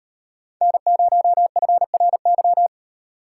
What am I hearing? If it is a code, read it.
Morse code, heard as a keyed tone, about 37 wpm, 700 Hz. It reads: N0FRY